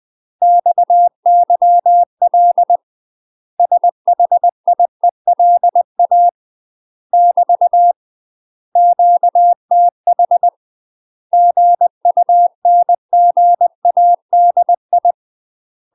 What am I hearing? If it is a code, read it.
Morse code, heard as a keyed tone, about 20 wpm, 700 Hz. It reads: XYL SHIELA <BT> QTH GUNGADI
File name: UK MASTER QSOs TRAINING / 20WPM / UK MASTER QSOs 0002